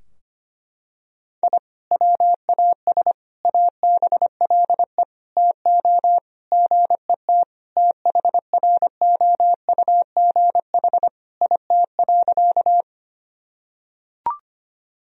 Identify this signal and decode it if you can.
Morse code, heard as a keyed tone, about 25 wpm, 700 Hz. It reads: I WAS ABLE TO GET THROUGH IT.
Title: Sentences from Top 1000 Words - Review Part 2 - 25wpm